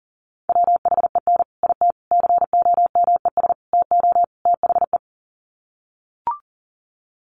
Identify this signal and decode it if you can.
Morse code, heard as a keyed tone, about 40 wpm, 700 Hz. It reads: WHEN IT COMES TO THE
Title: Sets of 5 Words 40wpm